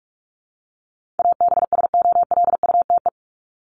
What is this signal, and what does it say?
Morse code, heard as a keyed tone, about 44 wpm, 700 Hz. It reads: ABSOLUTE
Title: LY Suffix - Top 100 Words with Base Words 45wpm